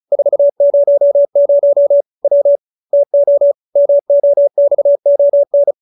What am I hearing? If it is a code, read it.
Morse code, heard as a keyed tone, about 35 wpm, 570 Hz. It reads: 400 W TO MOXON
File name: QSO310 35